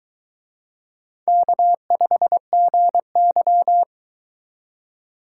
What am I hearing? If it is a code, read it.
Morse code, heard as a keyed tone, about 22 wpm, 700 Hz. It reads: K5GY